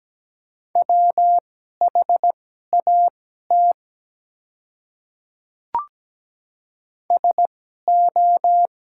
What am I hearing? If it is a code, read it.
Morse code, heard as a keyed tone, about 17 wpm, spaced as for 12 wpm, 700 Hz. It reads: WHAT SO
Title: 2x Character Spacing - Top 300 Words - 17wpm